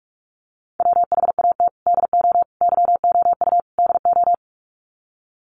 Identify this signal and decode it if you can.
Morse code, heard as a keyed tone, about 44 wpm, 700 Hz. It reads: WHAT DO YOU DO
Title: Sets of 4 Words 45wpm